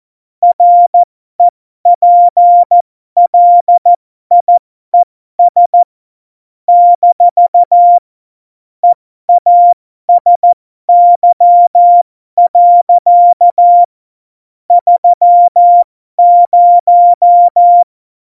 Morse code, heard as a keyed tone, about 14 wpm, spaced as for 12 wpm, 700 Hz. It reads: REPLIES - EASY. 30
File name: Morsekurs / Witze